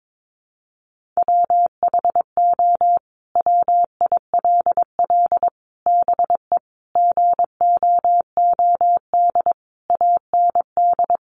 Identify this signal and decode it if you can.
Morse code, heard as a keyed tone, about 22 wpm, 700 Hz. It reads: WHO WILL BE GOOD AND
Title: Sentences from Top 1100 Words 22wpm